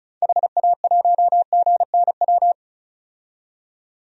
Morse code, heard as a keyed tone, about 36 wpm, 700 Hz. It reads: HA1GNW
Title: DXCC-100 Call Signs - 6 Characters - 35wpm